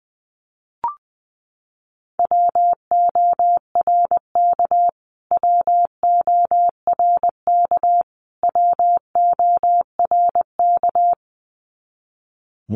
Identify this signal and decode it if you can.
Morse code, heard as a keyed tone, about 20 wpm, 700 Hz. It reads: WORK WORK WORK
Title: Top 500 Words Repeated 20wpm